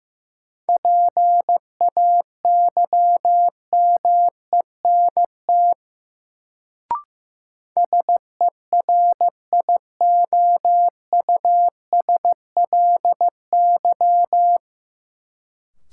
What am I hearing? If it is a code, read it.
Morse code, heard as a keyed tone, about 15 wpm, 700 Hz. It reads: PAYMENT SERIOUSLY
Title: Sentences from Top 1200 Words - New Word Drill - 15wpm